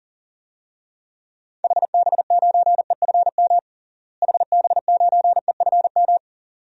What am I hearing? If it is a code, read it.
Morse code, heard as a keyed tone, about 40 wpm, 700 Hz. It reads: HB9EFM HB9EFM
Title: DXCC-100 Call Signs - 6 Characters - Repeated 2x - 40wpm